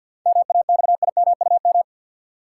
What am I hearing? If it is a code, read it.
Morse code, heard as a keyed tone, about 51 wpm, 700 Hz. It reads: MAXIMUM
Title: Sentences from Top 1300 Words - New Word Drill - 50wpm